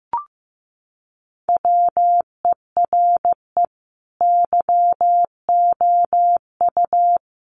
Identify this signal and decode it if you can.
Morse code, heard as a keyed tone, about 15 wpm, 700 Hz. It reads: WERE YOU
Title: Sentences from Top 1200 Words 15wpm